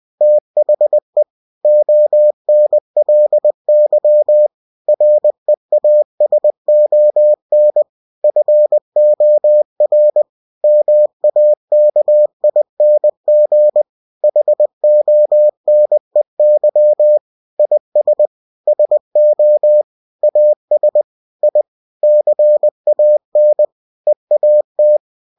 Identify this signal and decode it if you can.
Morse code, heard as a keyed tone, about 20 wpm, 600 Hz. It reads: THE ONLY REASON FOR MAKING HONEY IS SO AS I CAN EAT
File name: cw-books / winnie-the-pooh / winnie-20wpm-page-0002